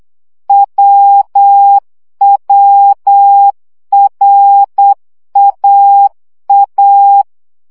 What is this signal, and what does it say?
Morse code, heard as a keyed tone, about 8 wpm, 800 Hz. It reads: WWRAA
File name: CW gyakorlas a r w 7 baud